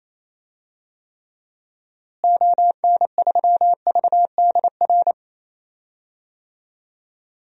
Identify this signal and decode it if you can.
Morse code, heard as a keyed tone, about 28 wpm, 700 Hz. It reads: ON3VDR
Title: DXCC-100 Call Signs - 6 Characters - Repeated 2x - 28wpm